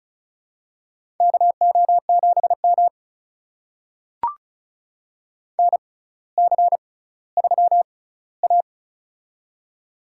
Morse code, heard as a keyed tone, about 36 wpm, spaced as for 18 wpm, 700 Hz. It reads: KO7M NC3A
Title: USA Call Signs - 4 Characters - 6x Character Spacing - 35wpm